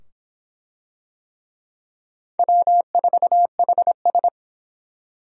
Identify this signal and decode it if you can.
Morse code, heard as a keyed tone, about 25 wpm, 700 Hz. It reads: W4HS